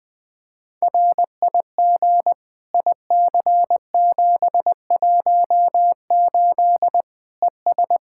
Morse code, heard as a keyed tone, about 20 wpm, 700 Hz. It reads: RIG IC718 ES
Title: Practice QSO - Exchange 2 - Info - 20wpm